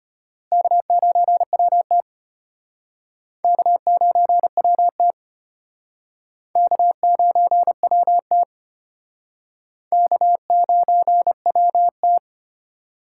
Morse code, heard as a keyed tone, about 32 wpm, 700 Hz. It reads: K9WT K9WT K9WT K9WT